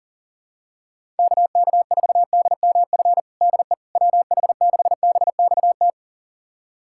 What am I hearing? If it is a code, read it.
Morse code, heard as a keyed tone, about 40 wpm, 700 Hz. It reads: KK4DMF DE WH6BXT
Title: USA Callsign DE Callsign 40wpm